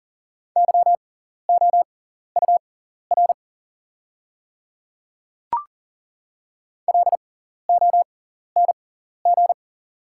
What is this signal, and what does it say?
Morse code, heard as a keyed tone, about 38 wpm, spaced as for 14 wpm, 690 Hz. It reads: YOUR LONG
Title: 6x Character Spacing - Top 100 Words - 40wpm